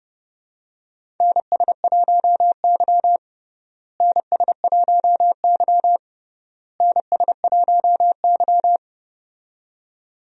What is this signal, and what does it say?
Morse code, heard as a keyed tone, about 30 wpm, 700 Hz. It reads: NS1Y NS1Y NS1Y
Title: USA Call Signs - 4 Characters - Repeated 3x - 30wpm